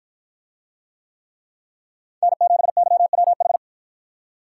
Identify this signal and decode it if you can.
Morse code, heard as a keyed tone, about 50 wpm, 700 Hz. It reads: N7YWH